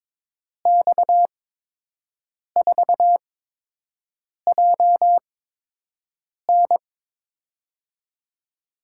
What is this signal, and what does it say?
Morse code, heard as a keyed tone, about 22 wpm, spaced as for 6 wpm, 700 Hz. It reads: X4JN
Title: USA Call Signs - 5 Characters - 8x Character Spacing - 22wpm